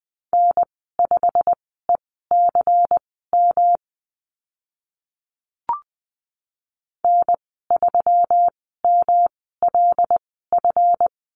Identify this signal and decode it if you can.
Morse code, heard as a keyed tone, about 20 wpm, spaced as for 14 wpm, 700 Hz. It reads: N5ECM N3MLF
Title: 161 Lesson 26g - Letter g - Call Signs 20wpm